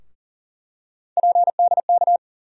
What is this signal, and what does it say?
Morse code, heard as a keyed tone, about 40 wpm, 700 Hz. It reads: PDK
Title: USA Call Sign Postfixes - 3 Letters 40wpm